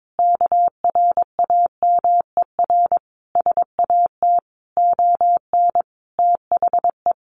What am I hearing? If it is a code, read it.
Morse code, heard as a keyed tone, about 22 wpm, 700 Hz. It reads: KRAMER SAT ON THE
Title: A Sound Of Thunder 22wpm